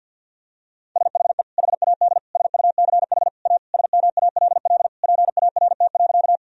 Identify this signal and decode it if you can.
Morse code, heard as a keyed tone, about 52 wpm, 700 Hz. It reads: SHE HAD SUCH A SMALL PART.
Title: Sentences from Top 1000 Words - Review Part 1 - 50wpm